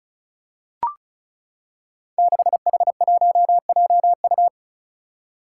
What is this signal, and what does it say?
Morse code, heard as a keyed tone, about 34 wpm, 700 Hz. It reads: 6H1JU
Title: Five Challenging Letter-Number Combinations 35wpm